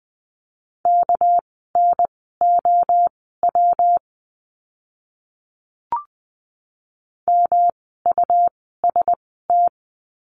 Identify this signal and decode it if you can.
Morse code, heard as a keyed tone, about 20 wpm, spaced as for 15 wpm, 700 Hz. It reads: KNOW MUST
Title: 237 Lesson 38f - Letter q - Words 20wpm